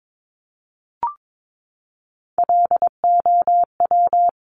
Morse code, heard as a keyed tone, about 22 wpm, 700 Hz. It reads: LOW
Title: Top Three Letter Combinations in Words 22wpm